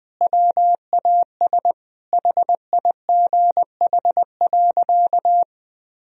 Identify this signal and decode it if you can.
Morse code, heard as a keyed tone, about 20 wpm, 700 Hz. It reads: WAS HIGH.
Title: Sentences from Top 1000 Words - Review Part 2 - 20wpm